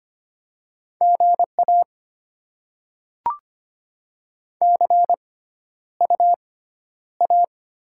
Morse code, heard as a keyed tone, about 25 wpm, spaced as for 11 wpm, 700 Hz. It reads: GA CUA
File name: Top 20 QSO Elements - 6x Character Spacing - 25wpm